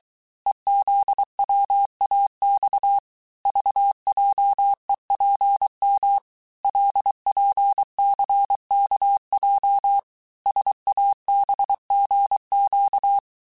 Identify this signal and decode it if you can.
Morse code, heard as a keyed tone, about 23 wpm, 800 Hz. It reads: EZWAX VJEPM LPCKJ SABGQ
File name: cours-du-mercredi / audios / cours-855-texte-chiffre-vitesse-22-mpm